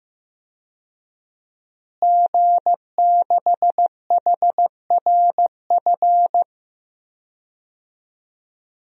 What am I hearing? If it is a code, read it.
Morse code, heard as a keyed tone, about 15 wpm, 700 Hz. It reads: G6HRF
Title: DXCC-100 Call Signs - 5 Characters - Repeated 2x - 15wpm